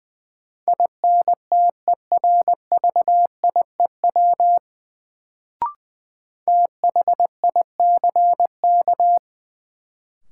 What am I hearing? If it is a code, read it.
Morse code, heard as a keyed tone, about 20 wpm, 700 Hz. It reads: INTERVIEW THICK
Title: Sentences from Top 1600 Words - New Word Drill - 20wpm